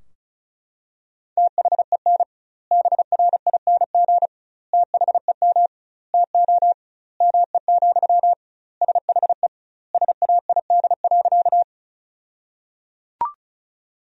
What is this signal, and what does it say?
Morse code, heard as a keyed tone, about 36 wpm, 700 Hz. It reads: THEN BRING THEM TO ME, SHE SAID.
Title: 2x Word-Spacing - Sentences from Top 500 Words - 35wpm